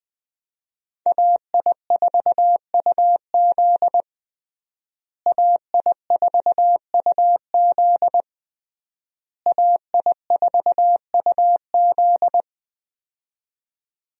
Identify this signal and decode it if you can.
Morse code, heard as a keyed tone, about 20 wpm, 700 Hz. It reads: AI4UZ AI4UZ AI4UZ
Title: USA Call Signs - 5 Characters - Repeated 3x - 20wpm